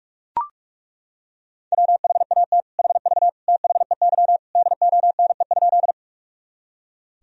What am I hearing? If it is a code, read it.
Morse code, heard as a keyed tone, about 41 wpm, 690 Hz. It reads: WHAT HV THEY DONE?
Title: Sentences from Top 500 Words and Abbreviations 45wpm